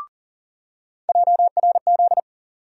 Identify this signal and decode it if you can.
Morse code, heard as a keyed tone, about 40 wpm, 700 Hz. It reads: JRZ
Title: USA Call Sign Postfixes - 3 Letters 40wpm